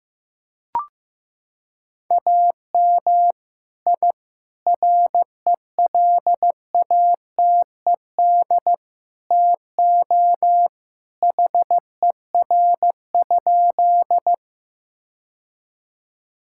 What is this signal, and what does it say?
Morse code, heard as a keyed tone, about 15 wpm, 700 Hz. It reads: AM I RELATED TO HER?
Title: Sentences from Top 1000 Words - Review Part 5 - 15wpm